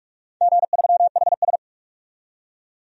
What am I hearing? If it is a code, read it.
Morse code, heard as a keyed tone, about 46 wpm, 700 Hz. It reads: G3HS
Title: DXCC-100 Call Signs - 4 Characters - 45wpm